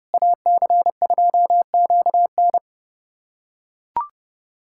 Morse code, heard as a keyed tone, about 30 wpm, 700 Hz. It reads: AC2QN
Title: USA Callsign DE Callsign - 5 Characters - 30wpm